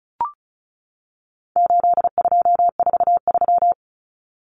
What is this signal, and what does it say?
Morse code, heard as a keyed tone, about 34 wpm, 700 Hz. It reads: 8243
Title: Common Cut Numbers - 4-Digit 35wpm